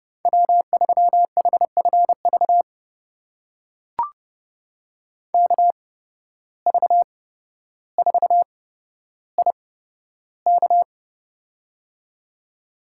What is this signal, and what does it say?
Morse code, heard as a keyed tone, about 29 wpm, spaced as for 13 wpm, 700 Hz. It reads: W3HFV KV4IK
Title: USA Call Signs - 5 Characters - 8x Character Spacing - 30wpm